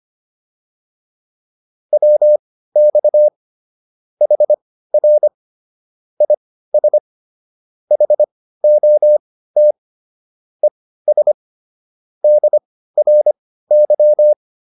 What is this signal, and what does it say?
Morse code, heard as a keyed tone, about 25 wpm, spaced as for 15 wpm, 600 Hz. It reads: WX HR IS HOT ES DRY
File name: QSO308 15